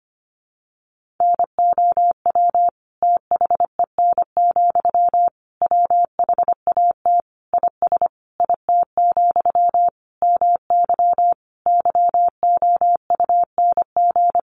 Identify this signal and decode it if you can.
Morse code, heard as a keyed tone, about 25 wpm, 700 Hz. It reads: NOW THEN, WHAT IS IT, MY YOUNG
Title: Sentences from Top 1100 Words 25wpm